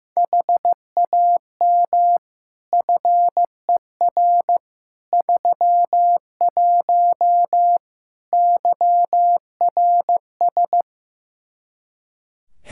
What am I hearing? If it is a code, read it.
Morse code, heard as a keyed tone, about 15 wpm, 700 Hz. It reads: HAM FER 31 YRS
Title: Practice QSO - Exchange 3 - More Info - 15wpm